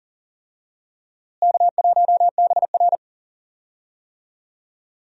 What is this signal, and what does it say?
Morse code, heard as a keyed tone, about 40 wpm, 700 Hz. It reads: K1BR
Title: USA Call Signs - 4 Characters - Repeated 3x - 40wpm